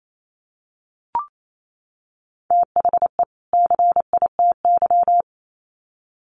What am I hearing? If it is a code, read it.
Morse code, heard as a keyed tone, about 28 wpm, 700 Hz. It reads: THE CITY
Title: Sets of 2 Words 28wpm